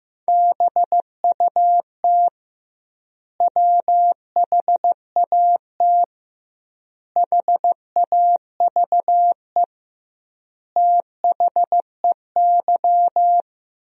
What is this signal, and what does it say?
Morse code, heard as a keyed tone, about 15 wpm, 700 Hz. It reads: BUT WHAT HAVE THEY
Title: 2x Word-Spacing - Sentences from Top 100 Words - 15wpm